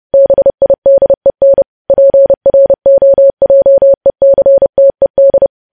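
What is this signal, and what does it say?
Morse code, heard as a keyed tone, about 30 wpm, 550 Hz. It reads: BIDEN PROJECTED
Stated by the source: News headlines sent in Morse code at 30.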